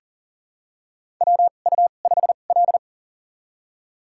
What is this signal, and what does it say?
Morse code, heard as a keyed tone, about 39 wpm, spaced as for 29 wpm, 690 Hz. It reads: WU5L
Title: USA Call Signs - 4 Characters - 2x Character Spacing - 40wpm